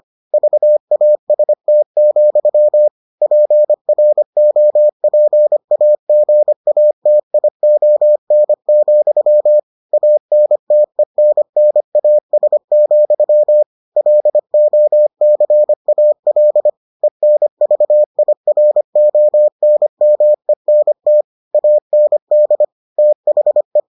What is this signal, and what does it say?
Morse code, heard as a keyed tone, about 25 wpm, 600 Hz. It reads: VAST, PROPAGATION, ANTENNAS, LOCAL ENVIRONMENT AND THE